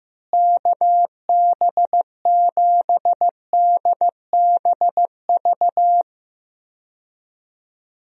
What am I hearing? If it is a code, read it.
Morse code, heard as a keyed tone, about 15 wpm, 700 Hz. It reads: KB7DBV
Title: USA Call Signs - 6 Characters - Repeated 2x - 15wpm